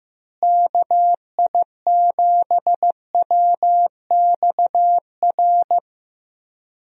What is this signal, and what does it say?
Morse code, heard as a keyed tone, about 15 wpm, 700 Hz. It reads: KI7WXR
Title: USA Call Signs - 6 Characters - Repeated 3x - 15wpm